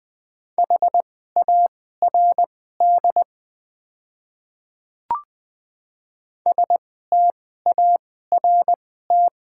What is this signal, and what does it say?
Morse code, heard as a keyed tone, about 20 wpm, spaced as for 14 wpm, 700 Hz. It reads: HARD START
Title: 2x Character Spacing - Top 500 Words - 20wpm